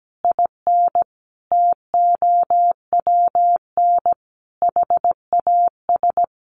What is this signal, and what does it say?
Morse code, heard as a keyed tone, about 17 wpm, 700 Hz. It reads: IN TOWN HAS
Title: Sentences from Top 1500 Words 17wpm